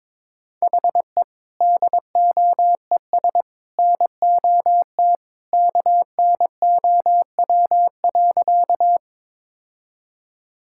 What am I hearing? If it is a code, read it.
Morse code, heard as a keyed tone, about 22 wpm, 700 Hz. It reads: HE DOES NOT KNOW.
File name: practice / 22wpm / Sentences with 3-Word Phrases - 22wpm